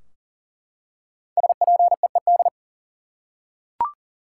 Morse code, heard as a keyed tone, about 40 wpm, 700 Hz. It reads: SPEED
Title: Five Letter Words - 40wpm